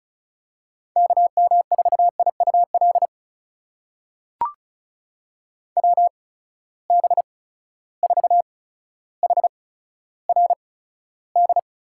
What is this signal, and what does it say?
Morse code, heard as a keyed tone, about 36 wpm, spaced as for 15 wpm, 700 Hz. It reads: KM4IUL WB4HRD
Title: USA Call Signs - 6 Characters - 8x Character Spacing - 35wpm